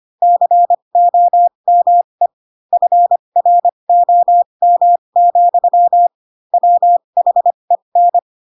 Morse code, heard as a keyed tone, about 25 wpm, spaced as for 22 wpm, 700 Hz. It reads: COME FROM, WHEN